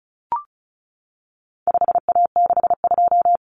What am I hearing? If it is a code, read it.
Morse code, heard as a keyed tone, about 34 wpm, 700 Hz. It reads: 5A62
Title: Common Cut Numbers - 4-Digit 35wpm